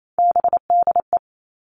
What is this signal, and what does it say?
Morse code, heard as a keyed tone, about 28 wpm, 700 Hz. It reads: BDE
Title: All Cut Numbers - 3-Digit 28wpm